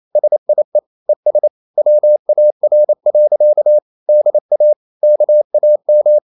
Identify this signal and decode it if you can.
Morse code, heard as a keyed tone, about 28 wpm, 600 Hz. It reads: SIE ES WAR. DA KAM